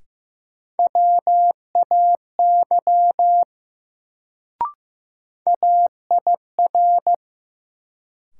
Morse code, heard as a keyed tone, about 15 wpm, 700 Hz. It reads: WAY AIR
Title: Sets of 2 Words - New Word Drill - 15wpm